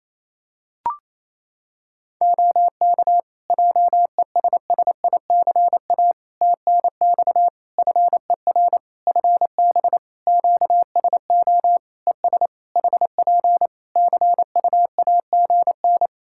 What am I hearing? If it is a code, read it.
Morse code, heard as a keyed tone, about 28 wpm, 700 Hz. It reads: OK JESSICA TNX FER FB QSO ES HP CUAGN
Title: Practice QSO - Exchange 4 - Ending - 28wpm